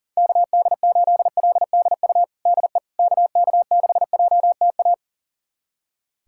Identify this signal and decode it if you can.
Morse code, heard as a keyed tone, about 40 wpm, 700 Hz. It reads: KD8LDU DE KK6JTA
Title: USA Callsign DE Callsign - 6 Characters - 40wpm